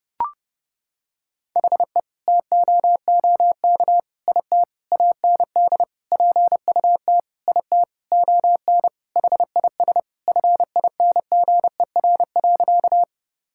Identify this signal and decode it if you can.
Morse code, heard as a keyed tone, about 30 wpm, 700 Hz. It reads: HE TOOK IT AND PUT IT ON HIS FINGER.
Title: Sentences from Top 1600 Words 30wpm